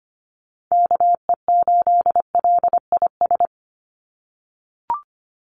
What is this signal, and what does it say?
Morse code, heard as a keyed tone, about 25 wpm, 690 Hz. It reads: KE8LIS
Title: USA Call Signs 25wpm